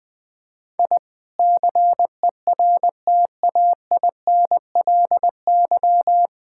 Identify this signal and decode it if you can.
Morse code, heard as a keyed tone, about 20 wpm, 700 Hz. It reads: I CERTAINLY
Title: Sentences from Top 1000 Words - Review Part 6 - 20wpm